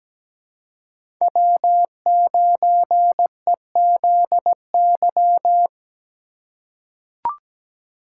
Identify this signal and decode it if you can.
Morse code, heard as a keyed tone, about 17 wpm, 700 Hz. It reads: W9EZY
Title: USA Call Signs - 5 Characters - 2x Character Spacing - 17wpm